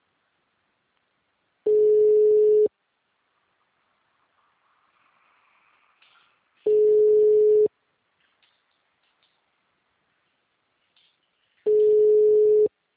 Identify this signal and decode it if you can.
Ringback tone (Europe, 425 Hz).
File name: out- -2027- - - .457